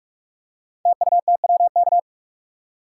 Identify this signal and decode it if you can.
Morse code, heard as a keyed tone, about 44 wpm, 700 Hz. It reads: TUTWK
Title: Code Groups - 5 Letters - 45wpm